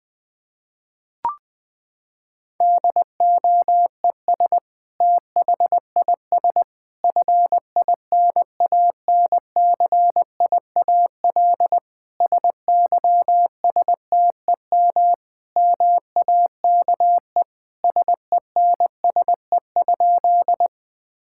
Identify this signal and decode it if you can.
Morse code, heard as a keyed tone, about 20 wpm, 700 Hz. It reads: DOES THIS FINANCIAL SYSTEM MAKE SENSE?
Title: Sentences from Top 1000 Words - Review Part 5 - 20wpm